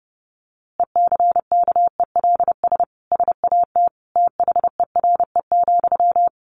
Morse code, heard as a keyed tone, about 30 wpm, 700 Hz. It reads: ECKELS SAT THERE,
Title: A Sound Of Thunder 30wpm